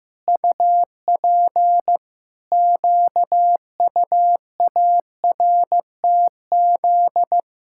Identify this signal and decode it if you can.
Morse code, heard as a keyed tone, about 15 wpm, 700 Hz. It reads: UP QUARTZ
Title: Pangram Challenge v2 - 15wpm